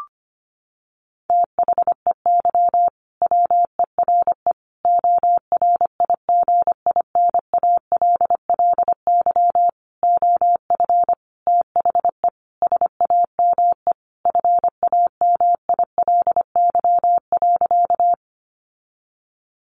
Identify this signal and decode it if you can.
Morse code, heard as a keyed tone, about 25 wpm, 700 Hz. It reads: THEY WERE ORIGINALLY OF THE SAME FAMILY.
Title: Sentences from Top 1500 Words 25wpm